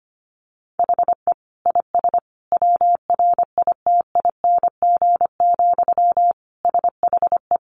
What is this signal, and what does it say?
Morse code, heard as a keyed tone, about 25 wpm, 700 Hz. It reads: HE IS WRITING, SHE
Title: Sentences from Top 800 Words 25wpm